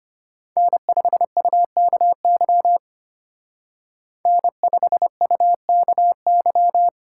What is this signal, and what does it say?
Morse code, heard as a keyed tone, about 33 wpm, 700 Hz. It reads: N5UKY N5UKY